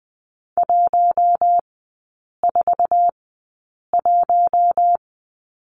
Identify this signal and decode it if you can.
Morse code, heard as a keyed tone, about 20 wpm, spaced as for 8 wpm, 700 Hz. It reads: 141
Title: 047 Lesson 9e - Number 4 - Three Characters 20wpm